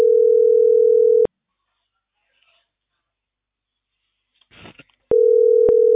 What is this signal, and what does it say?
Ringback tone (North America), 440 + 480 Hz.